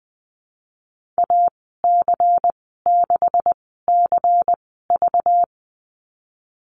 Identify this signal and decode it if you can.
Morse code, heard as a keyed tone, about 19 wpm, spaced as for 14 wpm, 700 Hz. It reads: AC6CV
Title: 193 Lesson 31g - Letter v - Call Signs 20wpm